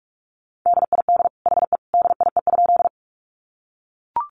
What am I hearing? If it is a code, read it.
Morse code, heard as a keyed tone, about 44 wpm, 700 Hz. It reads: DID HE DIE?
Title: Sentences from Top 1300 Words 45wpm